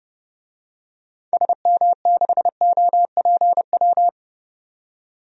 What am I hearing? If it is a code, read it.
Morse code, heard as a keyed tone, about 30 wpm, 700 Hz. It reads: SM6OPW
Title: DXCC-100 Call Signs - 6 Characters - 30wpm